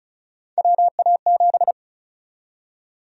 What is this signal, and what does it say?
Morse code, heard as a keyed tone, about 36 wpm, 700 Hz. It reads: WA7
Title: USA Call Sign Prefixes - 35wpm